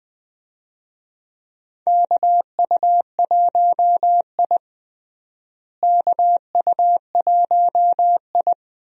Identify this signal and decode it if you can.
Morse code, heard as a keyed tone, about 20 wpm, 700 Hz. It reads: KU1I KU1I
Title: USA Call Signs - 4 Characters - Repeated 2x - 20wpm